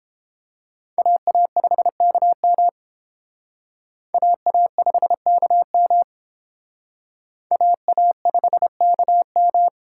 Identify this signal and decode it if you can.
Morse code, heard as a keyed tone, about 29 wpm, 700 Hz. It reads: AA5KM AA5KM AA5KM